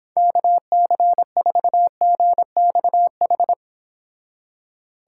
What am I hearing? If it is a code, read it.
Morse code, heard as a keyed tone, about 26 wpm, 700 Hz. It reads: KC4GXH